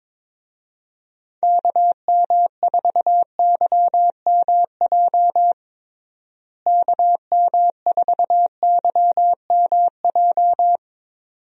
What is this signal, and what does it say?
Morse code, heard as a keyed tone, about 22 wpm, 700 Hz. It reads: KM4YMJ KM4YMJ
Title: USA Call Signs - 6 Characters - Repeated 2x - 22wpm